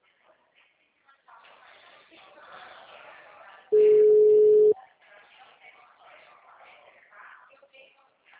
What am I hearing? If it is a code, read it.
Ringback tone (Europe, 425 Hz), with speech.
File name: out- -2008- - - .12560